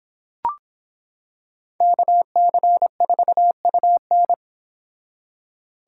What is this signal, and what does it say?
Morse code, heard as a keyed tone, about 26 wpm, 700 Hz. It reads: KC4UN